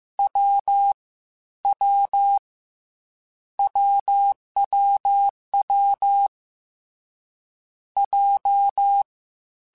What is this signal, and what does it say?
Morse code, heard as a keyed tone, about 15 wpm, 800 Hz. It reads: W W WWW J
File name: Difficult Combos / Difficult Combos-mp3s / AWJ1 @13